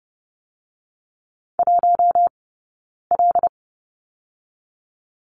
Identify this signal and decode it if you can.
Morse code, heard as a keyed tone, about 30 wpm, 700 Hz. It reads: 1 L
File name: 092 Lesson 16d - Letter c - Two Characters 30wpm